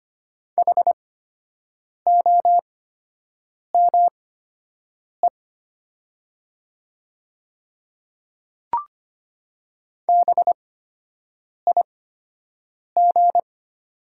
Morse code, heard as a keyed tone, about 25 wpm, spaced as for 7 wpm, 700 Hz. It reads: HOME BIG
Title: 8x Character Spacing - Top 200 Words - 25wpm